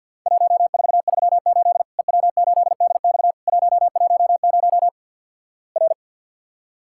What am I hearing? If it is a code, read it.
Morse code, heard as a keyed tone, about 53 wpm, 700 Hz. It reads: 1438 EW8DX 110 R
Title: Worked All Europe DX Contest - QTC Exchange - 50wpm